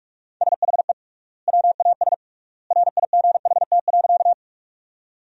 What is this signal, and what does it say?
Morse code, heard as a keyed tone, about 44 wpm, 700 Hz. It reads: SHE WAS RIGHT.
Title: 3x Word-Spacing - Sentences from Top 200 Words - 45wpm